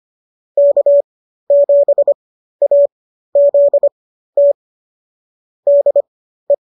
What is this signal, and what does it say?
Morse code, heard as a keyed tone, about 25 wpm, spaced as for 13 wpm, 590 Hz. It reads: K7AZT DE